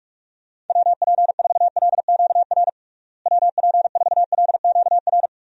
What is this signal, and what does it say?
Morse code, heard as a keyed tone, about 42 wpm, 700 Hz. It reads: WP4LQR WP4LQR
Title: USA Call Signs - 6 Characters - Repeated 3x - 45wpm